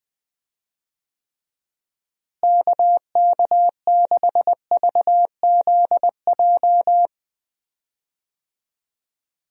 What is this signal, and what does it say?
Morse code, heard as a keyed tone, about 21 wpm, 700 Hz. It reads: KK6VZJ
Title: USA Call Signs - 6 Characters - Repeated 2x - 20wpm